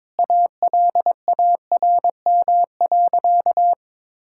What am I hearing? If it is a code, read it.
Morse code, heard as a keyed tone, about 22 wpm, 700 Hz. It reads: ALARM.
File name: practice / 22wpm / Sentences with 8-Word Phrases - 22wpm